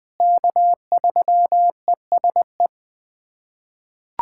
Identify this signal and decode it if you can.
Morse code, heard as a keyed tone, about 20 wpm, 700 Hz. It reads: K3ESE
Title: USA Callsign DE Callsign 20wpm